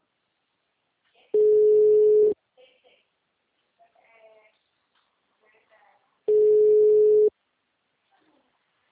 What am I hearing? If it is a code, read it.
Ringback tone (Europe, 425 Hz), with speech.